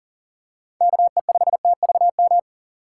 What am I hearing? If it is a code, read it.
Morse code, heard as a keyed tone, about 40 wpm, 700 Hz. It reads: KE5TVM
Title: USA Call Signs - 6 Characters - 2x Character Spacing - 40wpm